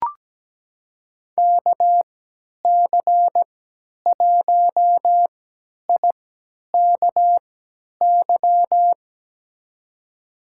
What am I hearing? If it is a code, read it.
Morse code, heard as a keyed tone, about 17 wpm, spaced as for 10 wpm, 700 Hz. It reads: KC1IKY
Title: USA Call Signs - 3x Character Spacing - 17wpm